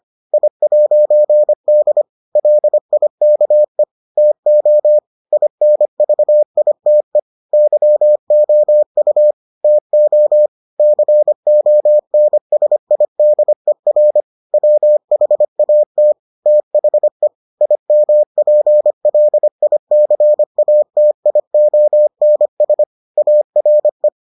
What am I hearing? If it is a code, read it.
Morse code, heard as a keyed tone, about 25 wpm, 600 Hz. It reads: I'D LIKE TO INVITE YOU TO CONSIDER WHAT THE IMPLICATIONS ARE